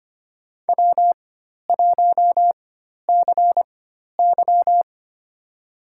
Morse code, heard as a keyed tone, about 25 wpm, spaced as for 12 wpm, 700 Hz. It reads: W1CY
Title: USA Call Signs - 4 Characters - 4x Character Spacing - 25wpm